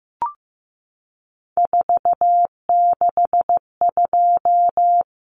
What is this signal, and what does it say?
Morse code, heard as a keyed tone, about 15 wpm, 700 Hz. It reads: 462
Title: Common Cut Numbers - 3-Digit 15wpm